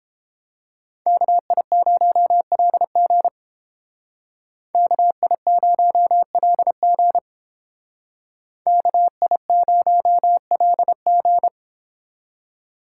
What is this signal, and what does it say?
Morse code, heard as a keyed tone, about 31 wpm, 700 Hz. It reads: KI0LG KI0LG KI0LG